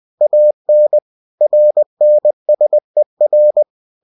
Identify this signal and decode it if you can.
Morse code, heard as a keyed tone, about 20 wpm, 600 Hz. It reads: AN RNSER
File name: cw-books / winnie-the-pooh / winnie-20wpm-page-0029